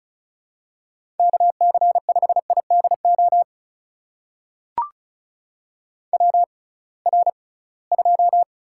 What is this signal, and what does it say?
Morse code, heard as a keyed tone, about 34 wpm, spaced as for 22 wpm, 700 Hz. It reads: KC5IDO WR2
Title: USA Call Signs - 6x Character Spacing - 35wpm